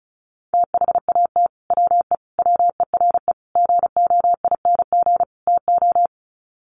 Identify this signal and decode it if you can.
Morse code, heard as a keyed tone, about 34 wpm, 700 Hz. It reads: THAT WE WERE GOING TO
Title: Sets of 5 Words Encore2 35wpm